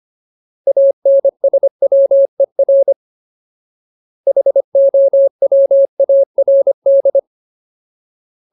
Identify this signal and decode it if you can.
Morse code, heard as a keyed tone, about 25 wpm, 550 Hz. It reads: ANSWER HOWARD
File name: Ultimate Word List / 25WPM Ultimate